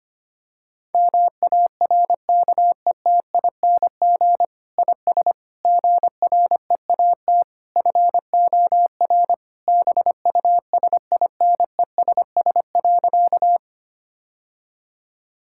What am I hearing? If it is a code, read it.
Morse code, heard as a keyed tone, about 25 wpm, 700 Hz. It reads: MARKETING IS GREAT FOR BUSINESS.
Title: Sentences from Top 1000 Words - Review Part 5 - 25wpm